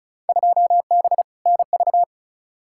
Morse code, heard as a keyed tone, about 34 wpm, 700 Hz. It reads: 2B NV
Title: Field Day Exchanges Encore - 35wpm